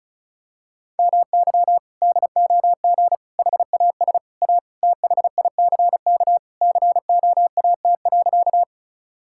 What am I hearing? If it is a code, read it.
Morse code, heard as a keyed tone, about 36 wpm, 700 Hz. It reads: MY DOG HAS A THICK COAT.
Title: Sentences from Top 1600 Words 35wpm